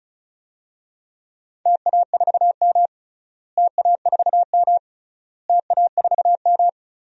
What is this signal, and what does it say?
Morse code, heard as a keyed tone, about 36 wpm, 700 Hz. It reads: TA4M TA4M TA4M
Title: DXCC-100 Call Signs - 4 Characters - Repeated 3x - 35wpm